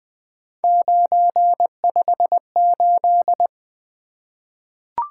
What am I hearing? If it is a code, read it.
Morse code, heard as a keyed tone, about 20 wpm, 700 Hz. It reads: 958
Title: Three Letter-Number Combinations - 20wpm